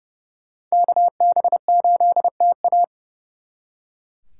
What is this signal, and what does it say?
Morse code, heard as a keyed tone, about 36 wpm, 700 Hz. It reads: KB8TA
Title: Speed Racing - USA 2x2 Call Signs 30wpm